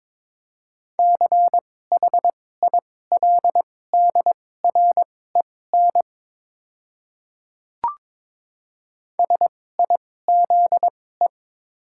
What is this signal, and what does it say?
Morse code, heard as a keyed tone, about 22 wpm, spaced as for 16 wpm, 700 Hz. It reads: CHILDREN SIZE
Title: 2x Character Spacing - Top 500 Words - 22wpm